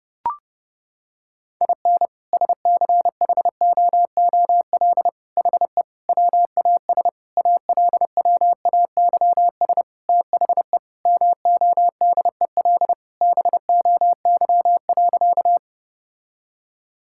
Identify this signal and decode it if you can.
Morse code, heard as a keyed tone, about 30 wpm, 700 Hz. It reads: IN SCHOOL HE WAS ALWAYS THE MODEL BOY.
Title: Sentences from Top 1000 Words - Review Part 5 - 30wpm